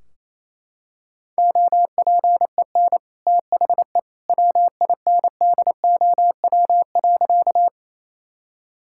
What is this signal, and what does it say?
Morse code, heard as a keyed tone, about 28 wpm, 700 Hz. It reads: OPEN THE WINDOW.
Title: Sentences from Top 1600 Words 28wpm